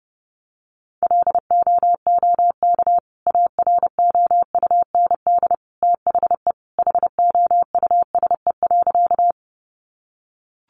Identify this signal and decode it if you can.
Morse code, heard as a keyed tone, about 30 wpm, 700 Hz. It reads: LOOK AROUND THE HOUSE.
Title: Sentences from Top 600 Words 30wpm